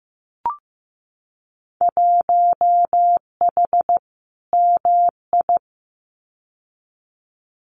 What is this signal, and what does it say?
Morse code, heard as a keyed tone, about 15 wpm, 700 Hz. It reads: 1H MI
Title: Winter Field Day Exchanges 15wpm